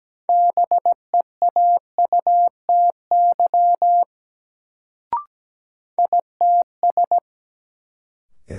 Morse code, heard as a keyed tone, about 17 wpm, 700 Hz. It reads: BEAUTY ITS
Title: Sentences from Top 600 Words - New Word Drill - 17wpm